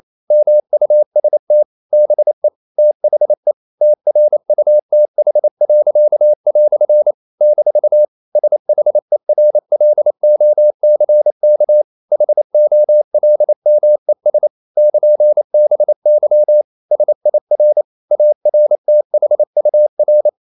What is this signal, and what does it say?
Morse code, heard as a keyed tone, about 28 wpm, 600 Hz. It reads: MUST BE THE TRUTH." - SHERLOCK HOLMES <KN>BY SIR ARTHUR
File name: CW / Practice / EnglishQuotations0228